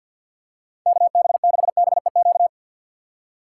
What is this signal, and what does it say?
Morse code, heard as a keyed tone, about 48 wpm, 700 Hz. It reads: KB6BEQ
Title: USA Call Signs - 6x Character Spacing - 50wpm